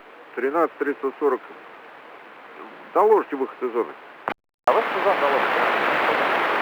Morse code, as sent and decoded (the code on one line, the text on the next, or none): none